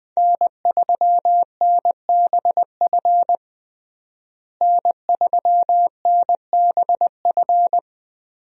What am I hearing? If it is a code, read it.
Morse code, heard as a keyed tone, about 21 wpm, 700 Hz. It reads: N3NBF N3NBF
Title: USA Call Signs - 5 Characters - Repeated 2x - 20wpm